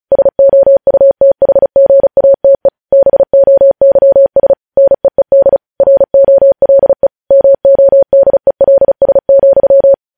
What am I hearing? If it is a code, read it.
Morse code, heard as a keyed tone, about 36 wpm, 550 Hz. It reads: SOUTHGATE BOYS NEED ROLE MODELS,
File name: podcast / News headlines-1x-35x35